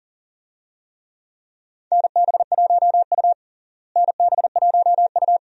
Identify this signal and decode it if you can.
Morse code, heard as a keyed tone, about 40 wpm, 700 Hz. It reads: NB1U NB1U
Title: USA Call Signs - 4 Characters - Repeated 2x - 40wpm